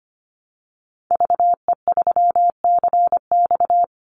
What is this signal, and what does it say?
Morse code, heard as a keyed tone, about 25 wpm, 700 Hz. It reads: VE3CX
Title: Call Signs Repeated 2x - 25wpm